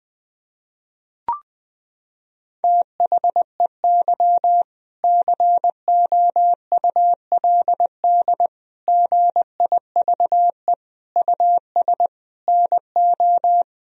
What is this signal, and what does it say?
Morse code, heard as a keyed tone, about 21 wpm, 700 Hz. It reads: THEY COULD GIVE US NO
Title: Sentences from Top 1000 Words - Review Part 3 - 20wpm